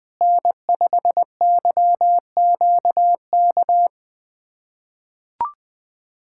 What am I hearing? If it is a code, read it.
Morse code, heard as a keyed tone, about 21 wpm, 700 Hz. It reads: N5YQK
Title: USA Callsign DE Callsign - 5 Characters - 20wpm